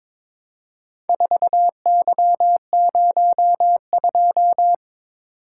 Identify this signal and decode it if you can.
Morse code, heard as a keyed tone, about 22 wpm, 700 Hz. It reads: 4Y02
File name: practice / 22wpm / Four Letter-Number Combinations - 22wpm